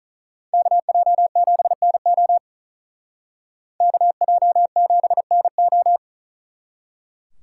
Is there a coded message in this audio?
Yes: Morse code from 0.5 to 6.0 s, heard as a keyed tone, about 45 wpm, 700 Hz. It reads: KJ7NO KJ7NO